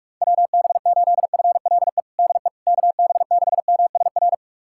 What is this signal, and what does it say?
Morse code, heard as a keyed tone, about 45 wpm, 700 Hz. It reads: WB8FLE DE KB6MSR
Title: USA Callsign DE Callsign - 6 Characters - 45wpm